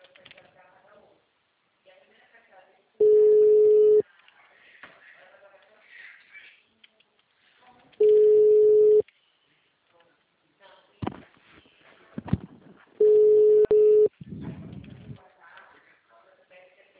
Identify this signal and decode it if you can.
Ringback tone (Europe, 425 Hz), with speech.
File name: out- -2043- - - .25337